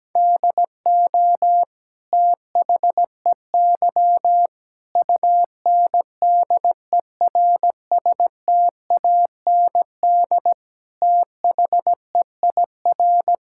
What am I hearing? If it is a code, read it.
Morse code, heard as a keyed tone, about 17 wpm, 700 Hz. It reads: DO THEY UNDERSTAND THEIR
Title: Sentences from Top 1000 Words - Review Part 6 - 17wpm